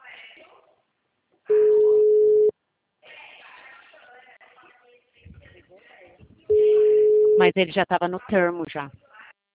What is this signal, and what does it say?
Ringback tone (Europe, 425 Hz), with speech.